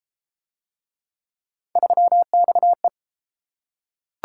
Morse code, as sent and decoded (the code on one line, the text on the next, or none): ...-- -..- .
3XE